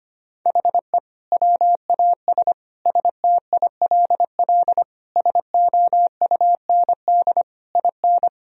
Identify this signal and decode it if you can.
Morse code, heard as a keyed tone, about 25 wpm, 700 Hz. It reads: HE WAS STILL SOUND IN
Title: Sentences from Top 1000 Words - Review Part 2 - 25wpm